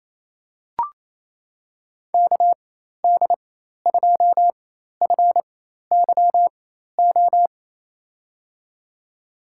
Morse code, heard as a keyed tone, about 28 wpm, spaced as for 13 wpm, 700 Hz. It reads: KD2FYO
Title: USA Call Signs - 4x Character Spacing - 28wpm